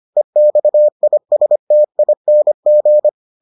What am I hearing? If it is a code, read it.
Morse code, heard as a keyed tone, about 25 wpm, 600 Hz. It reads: EXISTING